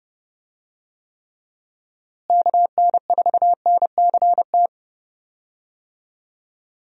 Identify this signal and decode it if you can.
Morse code, heard as a keyed tone, about 30 wpm, 700 Hz. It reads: KN4NCT
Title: USA Call Signs - 6 Characters - Repeated 2x - 30wpm